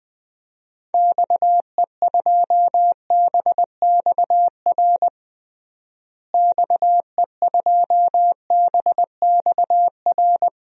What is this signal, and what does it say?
Morse code, heard as a keyed tone, about 21 wpm, 700 Hz. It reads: XE2BXR XE2BXR
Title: DXCC-100 Call Signs - 6 Characters - Repeated 3x - 20wpm